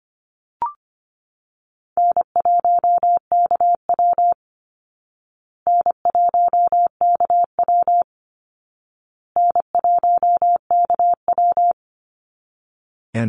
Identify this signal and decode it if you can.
Morse code, heard as a keyed tone, about 25 wpm, 700 Hz. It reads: N1KW N1KW N1KW
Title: Easy Call Signs Repeated 3x - 25wpm